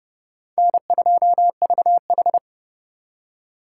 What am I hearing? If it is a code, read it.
Morse code, heard as a keyed tone, about 31 wpm, 700 Hz. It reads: N2VH
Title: USA Call Signs - 4 Characters - 4x Character Spacing - 30wpm